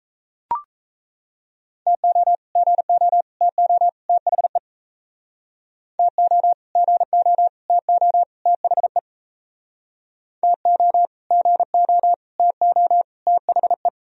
Morse code, heard as a keyed tone, about 36 wpm, 700 Hz. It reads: TO GO TO THE TO GO TO THE TO GO TO THE